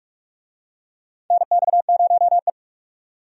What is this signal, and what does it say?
Morse code, heard as a keyed tone, about 46 wpm, 700 Hz. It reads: NX0E
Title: USA Call Signs - 4 Characters - 45wpm